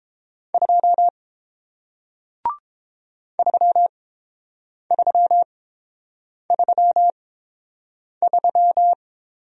Morse code, heard as a keyed tone, about 30 wpm, spaced as for 7 wpm, 700 Hz. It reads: 2 3333